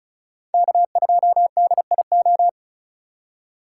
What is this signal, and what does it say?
Morse code, heard as a keyed tone, about 45 wpm, 700 Hz. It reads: K2DIO